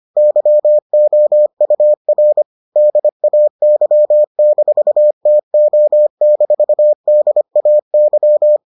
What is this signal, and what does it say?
Morse code, heard as a keyed tone, about 25 wpm, 600 Hz. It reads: YOUR DAY-TO-DAY